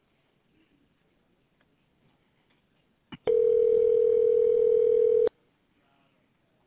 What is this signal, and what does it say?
Ringback tone (North America), 440 + 480 Hz.